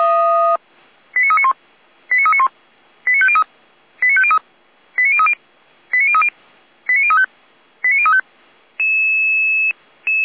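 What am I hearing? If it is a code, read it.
Five-tone selective call (ZVEI), heard as quick runs of five tones, tones of about 70 ms. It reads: 89391 89391 89503 89503 89030 89030 89035 89035
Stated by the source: Five-tone ZVEI selective call received by SDR.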